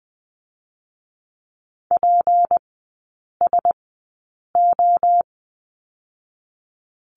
Morse code, heard as a keyed tone, about 20 wpm, 700 Hz. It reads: P S O
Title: 184 Lesson 30e - Letter b - Three Characters 20wpm